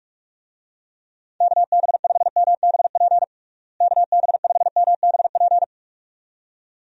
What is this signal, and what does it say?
Morse code, heard as a keyed tone, about 46 wpm, 700 Hz. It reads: KB5MBP KB5MBP
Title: USA Call Signs - 6 Characters - Repeated 2x - 45wpm